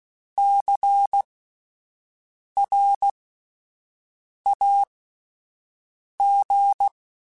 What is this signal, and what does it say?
Morse code, heard as a keyed tone, about 16 wpm, spaced as for 5 wpm, 800 Hz. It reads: CRAG
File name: Alphabetical / Alphabetical mp3s / Cs @5